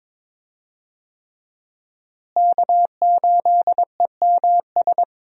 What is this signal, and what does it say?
Morse code, heard as a keyed tone, about 22 wpm, 690 Hz. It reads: K8EMS
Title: USA Call Signs - 5 Characters - Repeated 3x - 22wpm